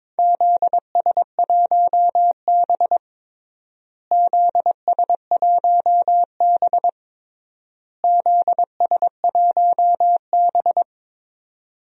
Morse code, heard as a keyed tone, about 22 wpm, 700 Hz. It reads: ZS1B ZS1B ZS1B
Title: DXCC-100 Call Signs - 4 Characters - Repeated 3x - 22wpm